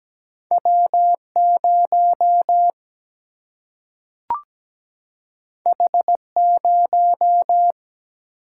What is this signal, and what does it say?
Morse code, heard as a keyed tone, about 17 wpm, 700 Hz. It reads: W0 H0
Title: Single-Letter Single-Number 17wpm